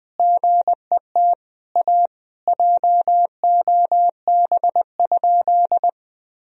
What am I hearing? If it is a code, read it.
Morse code, heard as a keyed tone, about 20 wpm, 700 Hz. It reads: GET A JOB?
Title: Sentences from Top 1000 Words - Review Part 3 - 20wpm